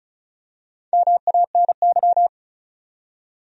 Morse code, heard as a keyed tone, about 36 wpm, 700 Hz. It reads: MANY
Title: Top 100 Words 35wpm